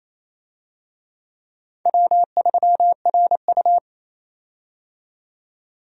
Morse code, heard as a keyed tone, about 28 wpm, 700 Hz. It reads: W3RU
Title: USA Call Signs - 4 Characters - 28wpm